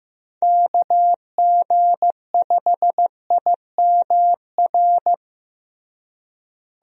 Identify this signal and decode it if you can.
Morse code, heard as a keyed tone, about 15 wpm, 700 Hz. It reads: KG5IMR
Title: USA Call Signs - 6 Characters - Repeated 2x - 15wpm